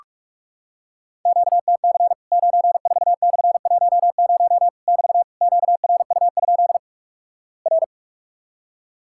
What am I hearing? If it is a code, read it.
Morse code, heard as a keyed tone, about 46 wpm, 700 Hz. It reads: QTC 94/10 <BT> QRU? R
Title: Worked All Europe DX Contest - QTC Exchange - 45wpm